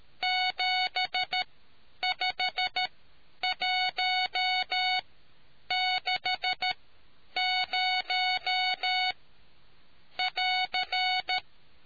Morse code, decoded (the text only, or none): 75160<AR>